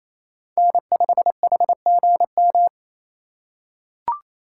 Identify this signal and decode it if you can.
Morse code, heard as a keyed tone, about 28 wpm, 700 Hz. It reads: N5HGM
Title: USA Call Signs - 5 Characters - 4x Character Spacing - 28wpm